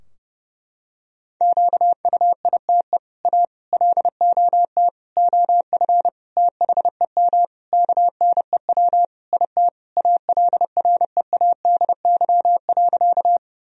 Morse code, heard as a keyed tone, about 29 wpm, 700 Hz. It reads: QUITE A LOT OF THEM KNEW IT ALREADY.
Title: Sentences from Top 1000 Words - Review Part 3 - 30wpm